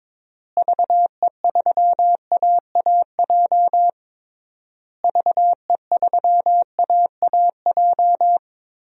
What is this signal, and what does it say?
Morse code, heard as a keyed tone, about 22 wpm, 700 Hz. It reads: VE3AAJ VE3AAJ
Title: DXCC-100 Call Signs - 6 Characters - Repeated 3x - 22wpm